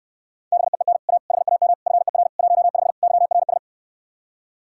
Morse code, heard as a keyed tone, about 66 wpm, 700 Hz. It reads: BEEN A HAM FER 15 YRS